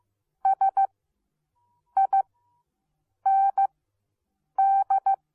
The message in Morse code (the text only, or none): SIND